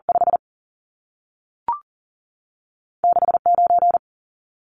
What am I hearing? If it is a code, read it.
Morse code, heard as a keyed tone, about 40 wpm, spaced as for 3 wpm, 700 Hz. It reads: 56 9
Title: Numbers - 2-Digit 40wpm